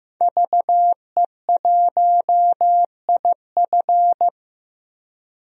Morse code, heard as a keyed tone, about 15 wpm, 700 Hz. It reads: VE1IF
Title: DXCC-100 Call Signs - 5 Characters - 15wpm